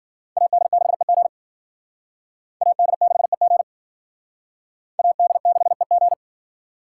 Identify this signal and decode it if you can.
Morse code, heard as a keyed tone, about 50 wpm, 700 Hz. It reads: AD6EG AD6EG AD6EG
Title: Speed Racing - USA 2x2 Call Signs 40wpm